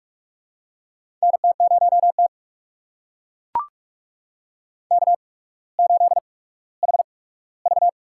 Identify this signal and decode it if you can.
Morse code, heard as a keyed tone, about 44 wpm, spaced as for 19 wpm, 700 Hz. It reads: NT0T K8HV
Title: USA Call Signs - 4 Characters - 8x Character Spacing - 45wpm